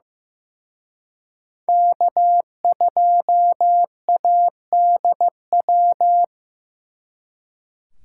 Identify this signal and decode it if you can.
Morse code, heard as a keyed tone, about 18 wpm, 700 Hz. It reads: K2ADW
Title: Speed Racing - USA 1x3 Call Signs 15wpm